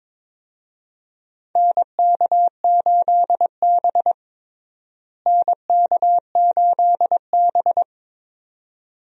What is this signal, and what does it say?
Morse code, heard as a keyed tone, about 22 wpm, 700 Hz. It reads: NK8B NK8B
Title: USA Call Signs - 4 Characters - Repeated 2x - 22wpm